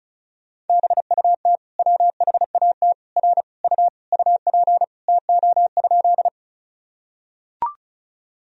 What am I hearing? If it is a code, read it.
Morse code, heard as a keyed tone, about 36 wpm, 700 Hz. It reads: BUT WHAT R U UP TO?
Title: Sentences from Top 100 Words and Abbreviations 35wpm